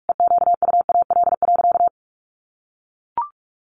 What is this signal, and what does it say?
Morse code, heard as a keyed tone, about 44 wpm, 700 Hz. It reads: EQUAL.
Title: Sentences from Top 1500 Words 45wpm